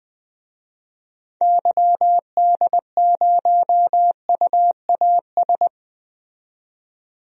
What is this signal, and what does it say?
Morse code, heard as a keyed tone, about 21 wpm, 700 Hz. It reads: YD0UAS
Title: DXCC-100 Call Signs - 6 Characters - 20wpm